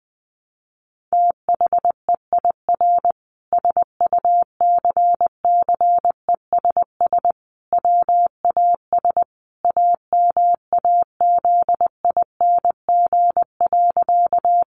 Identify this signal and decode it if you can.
Morse code, heard as a keyed tone, about 19 wpm, 700 Hz. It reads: THEIR SUCCESS WAS AMAZING.
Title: Sentences from Top 1500 Words 20wpm